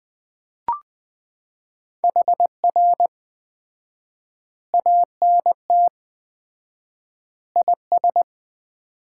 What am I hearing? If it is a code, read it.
Morse code, heard as a keyed tone, about 20 wpm, 700 Hz. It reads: HR ANT IS
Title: Common QSO Phrases - 4x Word-Spacing - 20wpm